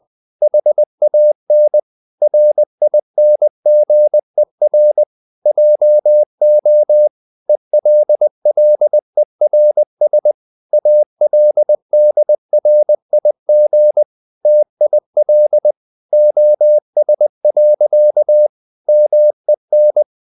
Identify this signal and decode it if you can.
Morse code, heard as a keyed tone, about 20 wpm, 600 Hz. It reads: HAN RINGER JO ELLERS ALDRIG TIL OS. MEN
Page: Ferie 20 wpm